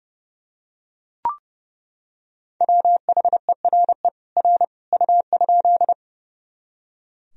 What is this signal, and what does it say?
Morse code, heard as a keyed tone, about 30 wpm, 700 Hz. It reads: WHERE R U?
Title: Sentences from Top 200 Words and Abbreviations 30wpm